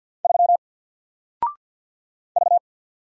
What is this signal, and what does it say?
Morse code, heard as a keyed tone, about 48 wpm, 700 Hz. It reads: 3 V
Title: Single Letter-Number 50wpm